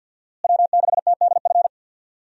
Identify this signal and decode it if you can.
Morse code, heard as a keyed tone, about 50 wpm, 700 Hz. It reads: W6TDF
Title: USA Call Signs - 5 Characters - Repeated 3x - 50wpm